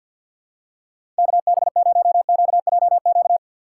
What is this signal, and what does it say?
Morse code, heard as a keyed tone, about 52 wpm, 700 Hz. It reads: KB0QJQ
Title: USA Call Signs - 6 Characters - 6x Character Spacing - 50wpm